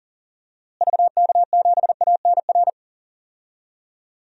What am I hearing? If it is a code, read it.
Morse code, heard as a keyed tone, about 40 wpm, 700 Hz. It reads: VK7ANR
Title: DXCC-100 Call Signs - 6 Characters - Repeated 2x - 40wpm